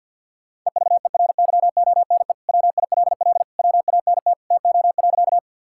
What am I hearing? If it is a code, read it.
Morse code, heard as a keyed tone, about 50 wpm, 700 Hz. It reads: EVERYONE WILL WANT TO.
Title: Sentences from Top 1000 Words - Review Part 5 - 50wpm